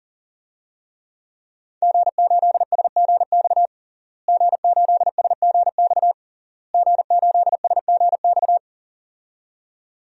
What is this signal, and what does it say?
Morse code, heard as a keyed tone, about 37 wpm, 700 Hz. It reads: G8SGX G8SGX G8SGX
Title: DXCC-100 Call Signs - 5 Characters - Repeated 3x - 40wpm